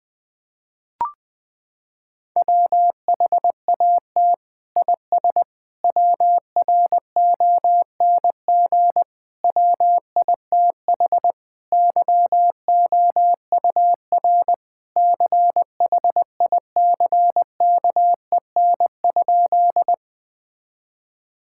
Morse code, heard as a keyed tone, about 20 wpm, 700 Hz. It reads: WHAT IS WRONG WITH YOUR CHICKEN?
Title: Sentences from Top 1000 Words - Review Part 5 - 20wpm